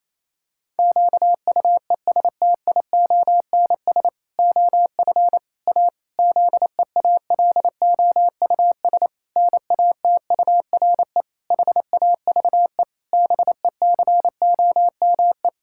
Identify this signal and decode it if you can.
Morse code, heard as a keyed tone, about 28 wpm, 700 Hz. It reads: QUESTIONS OF A ZEALOUS NATURE HAVE BECOME
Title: Pangram Challenge v2 - 28wpm